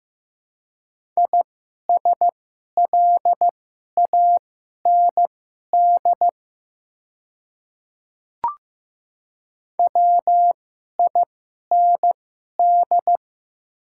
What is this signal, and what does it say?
Morse code, heard as a keyed tone, about 15 wpm, spaced as for 11 wpm, 700 Hz. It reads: ISLAND WIND
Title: 2x Character Spacing - Top 500 Words - 15wpm